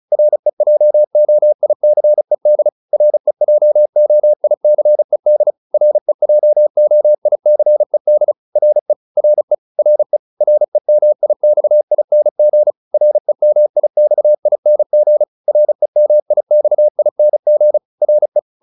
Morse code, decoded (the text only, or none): REJOICED REJOICED REJOICED RE RE RE REMIXING REMIXING REMIXING RE